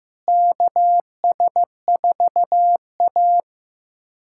Morse code, heard as a keyed tone, about 15 wpm, 700 Hz. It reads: KS4A
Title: USA Callsign DE Callsign - 4 Characters - 15wpm